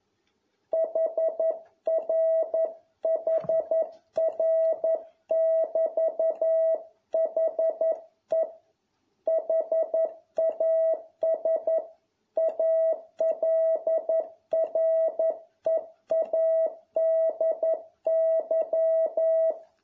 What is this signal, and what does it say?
Morse code, heard as a keyed tone, about 11 wpm, 650 Hz. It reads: HRHR<BT>HE HAS ALREADY